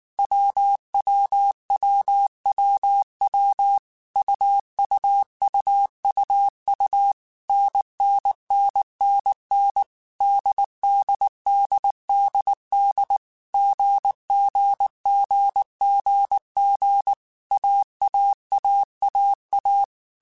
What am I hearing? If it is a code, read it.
Morse code, heard as a keyed tone, about 19 wpm, 800 Hz. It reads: WWWWW UUUUU NNNNN DDDDD GGGGG AAAAA